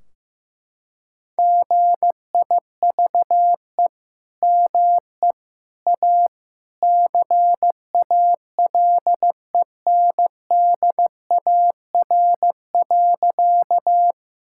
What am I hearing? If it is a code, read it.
Morse code, heard as a keyed tone, about 15 wpm, 700 Hz. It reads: GIVE ME A CALENDAR.
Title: Sentences from Top 1600 Words 15wpm